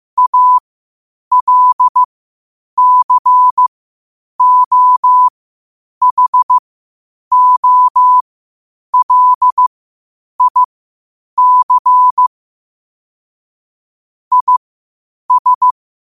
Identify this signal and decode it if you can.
Morse code, heard as a keyed tone, about 15 wpm, spaced as for 8 wpm, 1000 Hz. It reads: ALCOHOLIC IS
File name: qotdfiles / 8wpm